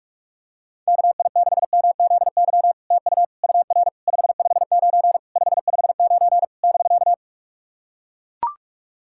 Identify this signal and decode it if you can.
Morse code, heard as a keyed tone, about 44 wpm, 700 Hz. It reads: KI6MZY TU UR 559 559 <BK>
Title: SOTA QSO Practice 45wpm